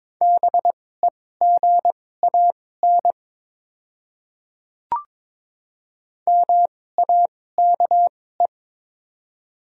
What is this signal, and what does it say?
Morse code, heard as a keyed tone, about 22 wpm, spaced as for 16 wpm, 700 Hz. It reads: BEGAN MAKE
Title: 2x Character Spacing - Top 500 Words - 22wpm